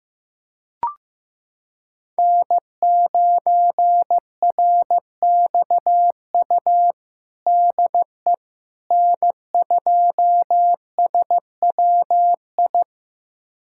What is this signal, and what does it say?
Morse code, heard as a keyed tone, about 15 wpm, 700 Hz. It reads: N9RXU DE N2SWI
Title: USA Callsign DE Callsign - 5 Characters - 15wpm